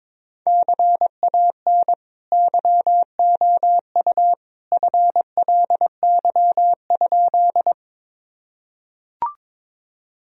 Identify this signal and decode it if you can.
Morse code, heard as a keyed tone, about 22 wpm, 700 Hz. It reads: CAN YOU FLY?
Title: Sentences from Top 1000 Words - Review Part 2 - 22wpm